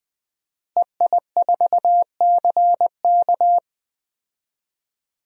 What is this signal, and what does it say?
Morse code, heard as a keyed tone, about 20 wpm, 700 Hz. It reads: EI4CK
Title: DXCC-100 Call Signs - 5 Characters - Repeated 3x - 20wpm